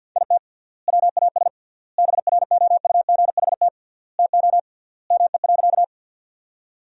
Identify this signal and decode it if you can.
Morse code, heard as a keyed tone, about 50 wpm, 700 Hz. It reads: IT WAS BROUGHT TO ME.
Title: 3x Word-Spacing - Sentences from Top 500 Words - 50wpm